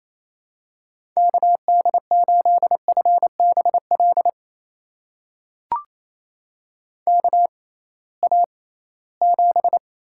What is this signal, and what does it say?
Morse code, heard as a keyed tone, about 28 wpm, spaced as for 18 wpm, 700 Hz. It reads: KD8FBL KA7
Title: USA Call Signs - 6 Characters - 6x Character Spacing - 28wpm